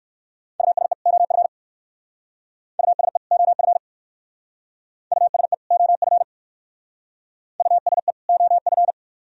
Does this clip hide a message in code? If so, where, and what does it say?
Yes: Morse code from 0.6 to 8.9 s, heard as a keyed tone, about 64 wpm, 700 Hz. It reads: USE OF USE OF USE OF USE OF